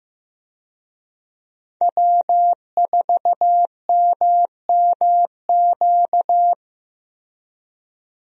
Morse code, heard as a keyed tone, about 15 wpm, 700 Hz. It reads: W4MMQ
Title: USA Call Signs - 5 Characters - Repeated 3x - 15wpm